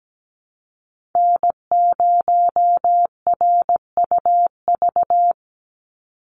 Morse code, heard as a keyed tone, about 17 wpm, 700 Hz. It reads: N0RUV
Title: USA Call Signs 17wpm